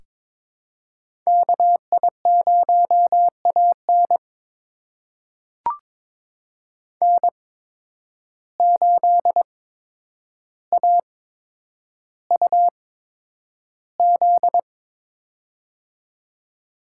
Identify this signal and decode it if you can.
Morse code, heard as a keyed tone, about 22 wpm, spaced as for 9 wpm, 700 Hz. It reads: KI0AN N8AUZ
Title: USA Call Signs - 8x Character Spacing - 22wpm